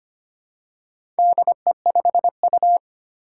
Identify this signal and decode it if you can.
Morse code, heard as a keyed tone, about 25 wpm, 700 Hz. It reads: DE5U
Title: Four Challenging Letter-Number Combinations 25wpm